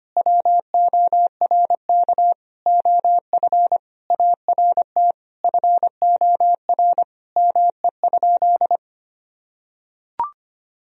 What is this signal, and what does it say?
Morse code, heard as a keyed tone, about 25 wpm, 700 Hz. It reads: WORK OF ART FOR ME?
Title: Sentences from Top 1000 Words - Review Part 3 - 25wpm